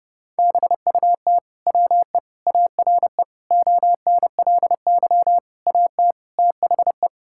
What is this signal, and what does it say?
Morse code, heard as a keyed tone, about 32 wpm, 700 Hz. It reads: BUT WE ARE ONLY AT THE
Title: Sentences from Top 1000 Words - Review Part 5 - 30wpm